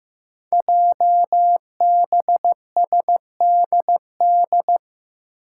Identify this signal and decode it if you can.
Morse code, heard as a keyed tone, about 15 wpm, 700 Hz. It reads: JBSDD
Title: Five Challenging Letter-Number Combinations 15wpm